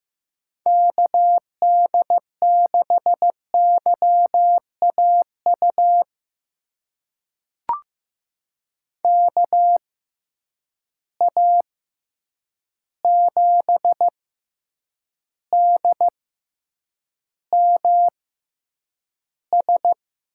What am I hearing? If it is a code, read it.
Morse code, heard as a keyed tone, about 15 wpm, spaced as for 8 wpm, 700 Hz. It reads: KD6YAU KA7DMS
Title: USA Call Signs - 6 Characters - 6x Character Spacing - 15wpm